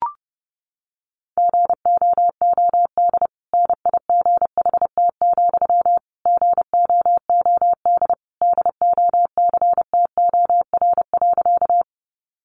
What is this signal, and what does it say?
Morse code, heard as a keyed tone, about 30 wpm, 700 Hz. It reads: GOOD NIGHT, GOOD DOCTOR.
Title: Sentences from Top 1500 Words 30wpm